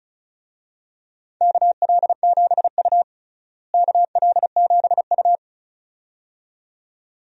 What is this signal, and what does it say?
Morse code, heard as a keyed tone, about 34 wpm, 700 Hz. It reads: KL7U KL7U
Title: DXCC-100 Call Signs - 4 Characters - Repeated 2x - 35wpm